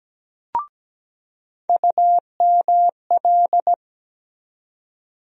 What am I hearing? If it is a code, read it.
Morse code, heard as a keyed tone, about 17 wpm, 700 Hz. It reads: UML
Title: USA Call Sign Postfixes - 3 Letters 17wpm